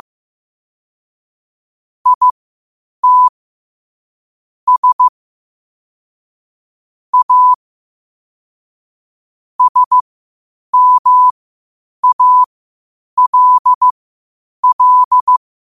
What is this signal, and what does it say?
Morse code, heard as a keyed tone, about 15 wpm, spaced as for 8 wpm, 1000 Hz. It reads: IT S A SMALL
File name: qotdfiles / 8wpm